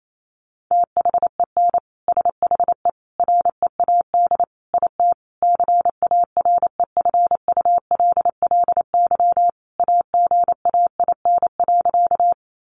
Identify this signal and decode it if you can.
Morse code, heard as a keyed tone, about 28 wpm, 700 Hz. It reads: THEN SHE READ IT CAREFULLY AGAIN.
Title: Sentences from Top 1300 Words 28wpm